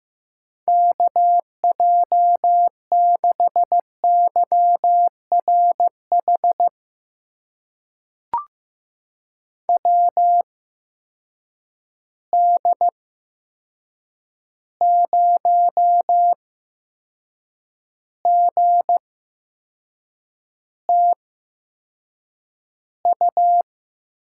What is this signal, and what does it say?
Morse code, heard as a keyed tone, about 15 wpm, spaced as for 6 wpm, 700 Hz. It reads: KJ6YRH WD0GTU
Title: USA Call Signs - 6 Characters - 8x Character Spacing - 15wpm